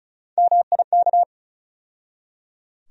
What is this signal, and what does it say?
Morse code, heard as a keyed tone, about 36 wpm, 700 Hz. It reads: MIK
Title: USA Call Sign Postfixes - 3 Letters 35wpm